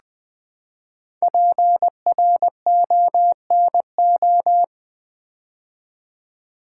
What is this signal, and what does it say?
Morse code, heard as a keyed tone, about 20 wpm, 700 Hz. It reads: PRONO
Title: Build A Word - 5 Letters At A Time - 10-Letter Words - 20wpm